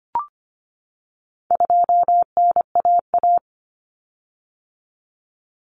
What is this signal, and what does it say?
Morse code, heard as a keyed tone, about 25 wpm, 700 Hz. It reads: 2NAA
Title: Common Cut Numbers - 4-Digit 25wpm